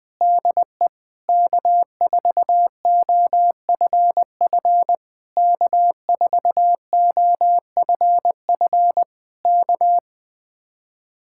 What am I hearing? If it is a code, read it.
Morse code, heard as a keyed tone, about 21 wpm, 700 Hz. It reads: DE K4OFF K4OFF K